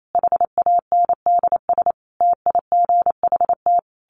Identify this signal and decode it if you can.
Morse code, heard as a keyed tone, about 28 wpm, 700 Hz. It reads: HANDS TIGHT
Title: A Sound Of Thunder 28wpm